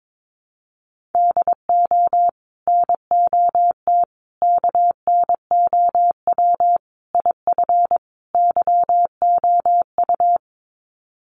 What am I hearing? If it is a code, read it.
Morse code, heard as a keyed tone, about 22 wpm, 700 Hz. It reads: DO NOT KNOW IF YOU
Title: Sets of 5 Words 22wpm